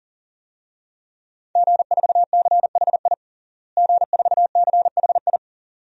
Morse code, heard as a keyed tone, about 40 wpm, 700 Hz. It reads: G4CHI G4CHI
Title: DXCC-100 Call Signs - 5 Characters - Repeated 2x - 40wpm